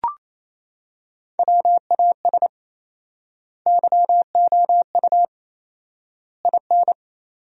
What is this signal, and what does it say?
Morse code, heard as a keyed tone, about 28 wpm, 700 Hz. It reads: WAS YOU IN
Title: 4x Word-Spacing - Sentences from Top 300 Words - 28wpm